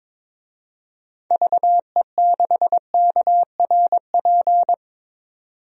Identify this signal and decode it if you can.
Morse code, heard as a keyed tone, about 22 wpm, 700 Hz. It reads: VE6KRP
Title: DXCC-100 Call Signs - 6 Characters - 22wpm